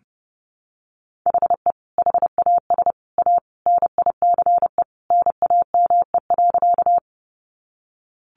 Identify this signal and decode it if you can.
Morse code, heard as a keyed tone, about 30 wpm, 700 Hz. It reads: HE HAS A NICE NAME.
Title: Sentences from Top 1100 Words 30wpm